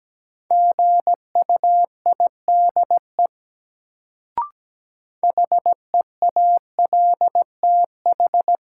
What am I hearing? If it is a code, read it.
Morse code, heard as a keyed tone, about 17 wpm, 700 Hz. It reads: GUIDE HEALTH
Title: Sentences from Top 600 Words - New Word Drill - 17wpm